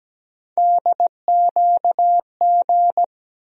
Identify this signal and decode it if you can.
Morse code, heard as a keyed tone, about 17 wpm, 700 Hz. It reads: DQG
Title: USA Call Sign Postfixes - 3 Letters 17wpm